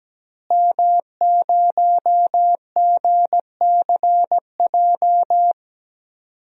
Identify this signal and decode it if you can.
Morse code, heard as a keyed tone, about 17 wpm, 700 Hz. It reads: M0GCJ
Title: DXCC-100 Call Signs - 5 Characters - Repeated 2x - 17wpm